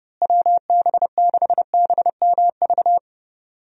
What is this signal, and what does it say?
Morse code, heard as a keyed tone, about 32 wpm, 700 Hz. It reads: WB6BMV
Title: USA Call Signs - 6 Characters - Repeated 2x - 30wpm